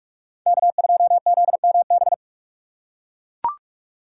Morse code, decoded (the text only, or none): K2ZMB